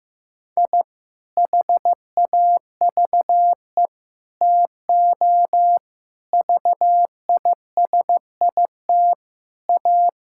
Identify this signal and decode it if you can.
Morse code, heard as a keyed tone, about 15 wpm, 700 Hz. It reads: I HAVE TO VISIT A
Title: Sentences from Top 1000 Words - Review Part 6 - 15wpm